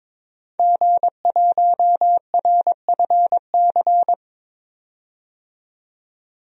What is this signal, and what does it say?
Morse code, heard as a keyed tone, about 22 wpm, 700 Hz. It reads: G1RFC
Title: DXCC-100 Call Signs - 5 Characters - 22wpm